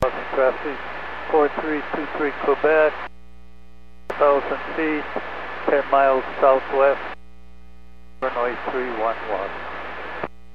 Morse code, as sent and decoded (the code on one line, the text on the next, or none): none